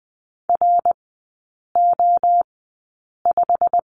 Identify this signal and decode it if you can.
Morse code, heard as a keyed tone, about 20 wpm, 690 Hz. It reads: R O 5
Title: 145 Lesson 24e - Letter y - Three Characters 20wpm